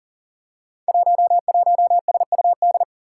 Morse code, heard as a keyed tone, about 41 wpm, 700 Hz. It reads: 11SUD
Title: Five Challenging Letter-Number Combinations 40wpm